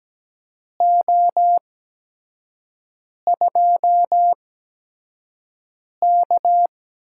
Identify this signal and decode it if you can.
Morse code, heard as a keyed tone, about 17 wpm, spaced as for 5 wpm, 700 Hz. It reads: O2K
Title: USA Call Signs - 8x Character Spacing - 17wpm